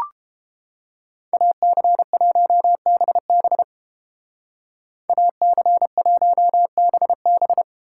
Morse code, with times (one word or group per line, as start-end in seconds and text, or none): 1.3-3.6 AC1BB
5.1-7.6 AC1BB